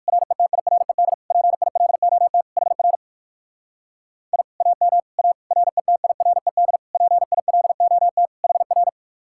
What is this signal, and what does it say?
Morse code, heard as a keyed tone, about 57 wpm, 700 Hz. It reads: RETIRED PILOT HR I AM A RETIRED PILOT HR